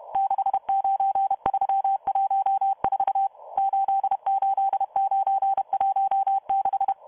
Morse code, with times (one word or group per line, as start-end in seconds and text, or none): 0.2-3.3 69314
3.6-6.9 88916